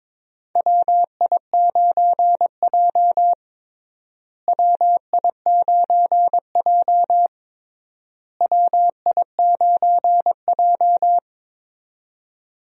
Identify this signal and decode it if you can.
Morse code, heard as a keyed tone, about 22 wpm, 700 Hz. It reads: WI9J WI9J WI9J
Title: USA Call Signs - 4 Characters - Repeated 3x - 22wpm